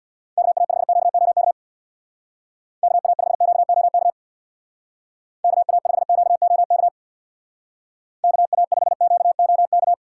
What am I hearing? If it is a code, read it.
Morse code, heard as a keyed tone, about 61 wpm, 700 Hz. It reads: KA5QOK KA5QOK KA5QOK KA5QOK
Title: Speed Racing - USA 2x3 Call Signs 50wpm